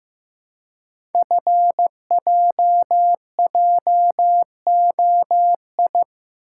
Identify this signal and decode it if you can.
Morse code, heard as a keyed tone, about 15 wpm, 700 Hz. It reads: FJJOI
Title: Code Groups - 5 Letters - 15wpm